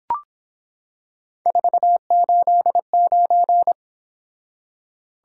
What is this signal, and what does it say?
Morse code, heard as a keyed tone, about 25 wpm, 700 Hz. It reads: 489